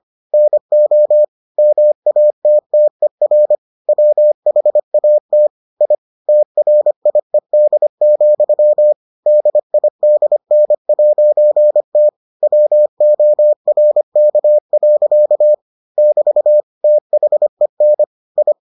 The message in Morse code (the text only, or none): NO MATTER WHAT I TRIED, DIDN'T WORK. <BT> THEN I